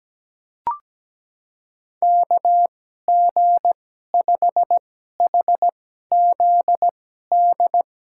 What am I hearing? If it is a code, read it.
Morse code, heard as a keyed tone, about 17 wpm, spaced as for 12 wpm, 700 Hz. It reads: KG5HZD
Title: USA Call Signs - 6 Characters - 2x Character Spacing - 17wpm